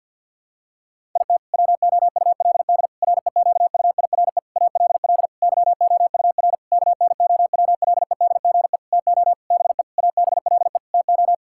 Morse code, heard as a keyed tone, about 50 wpm, 700 Hz. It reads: IT WOULD REQUIRE ALL YOUR KNOWLEDGE TO BE ABLE TO
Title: Sentences from Top 1000 Words - Review Part 6 - 50wpm